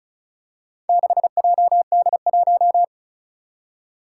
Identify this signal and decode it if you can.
Morse code, heard as a keyed tone, about 36 wpm, 700 Hz. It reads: 6JD1
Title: Four Challenging Letter-Number Combinations 35wpm